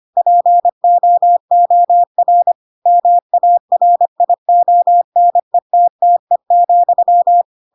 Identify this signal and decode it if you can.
Morse code, heard as a keyed tone, about 25 wpm, 700 Hz. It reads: POOR MARIONETTE,
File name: Pinocchio / Pinocchio ch7 25 25 ik1hge